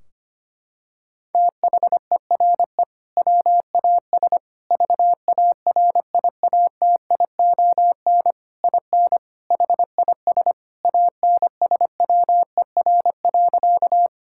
Morse code, heard as a keyed tone, about 25 wpm, 700 Hz. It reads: THERE WAS VARIATION IN HIS ANSWER.
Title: Sentences from Top 1600 Words 25wpm